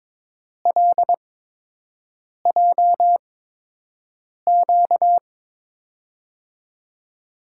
Morse code, heard as a keyed tone, about 22 wpm, spaced as for 6 wpm, 700 Hz. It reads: LJQ
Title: USA Call Signs - 8x Character Spacing - 22wpm